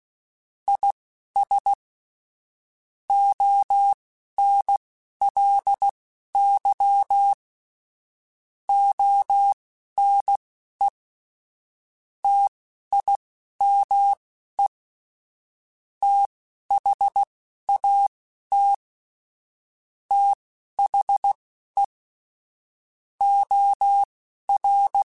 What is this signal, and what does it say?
Morse code, heard as a keyed tone, about 16 wpm, spaced as for 12 wpm, 800 Hz. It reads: IS ONLY ONE TIME THAT THE OR
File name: Technical Subjects / Technical Subjects-mp3s / OR Gate @10